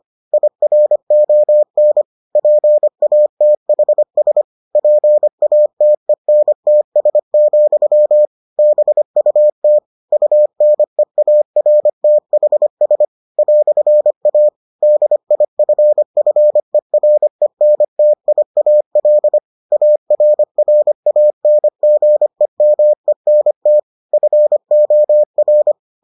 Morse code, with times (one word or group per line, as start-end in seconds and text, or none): 0.3-2.0 IRON
2.4-4.4 PATHS
4.8-8.3 PATENTS,
8.6-9.8 BUT
10.1-13.0 UNEARTHS
13.4-14.5 "A
14.8-19.4 DIFFERENTIAL
19.7-23.8 ARRANGEMENT
24.1-25.7 FOR